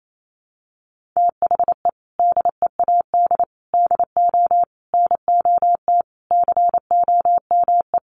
Morse code, heard as a keyed tone, about 28 wpm, 700 Hz. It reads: THE DEAD DO NOT COME
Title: Sentences from Top 1500 Words 28wpm